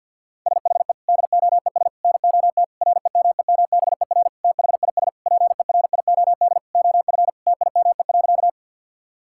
Morse code, heard as a keyed tone, about 51 wpm, 690 Hz. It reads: SHE DOES NOT REMEMBER THIS PERIOD OF TIME.
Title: Sentences from Top 1000 Words - Review Part 5 - 50wpm